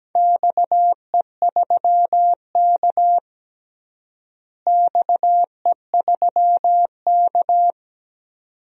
Morse code, heard as a keyed tone, about 17 wpm, 700 Hz. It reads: XE3K XE3K
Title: DXCC-100 Call Signs - 4 Characters - Repeated 2x - 17wpm